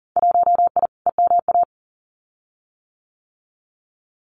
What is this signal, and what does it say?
Morse code, heard as a keyed tone, about 40 wpm, 700 Hz. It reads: 1I EMA
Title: Winter Field Day Exchanges Encore 40wpm